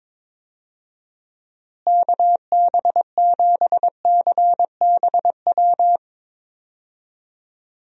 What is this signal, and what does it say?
Morse code, heard as a keyed tone, about 22 wpm, 700 Hz. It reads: KB7CBW
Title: USA Call Signs - 6 Characters - Repeated 3x - 22wpm